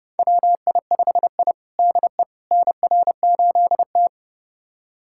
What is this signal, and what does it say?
Morse code, heard as a keyed tone, about 30 wpm, 700 Hz. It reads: WI5I DE NR8T
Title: USA Callsign DE Callsign - 4 Characters - 30wpm